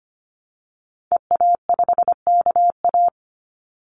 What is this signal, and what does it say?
Morse code, heard as a keyed tone, about 25 wpm, 700 Hz. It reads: EA5KA
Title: Call Signs Repeated 2x - 25wpm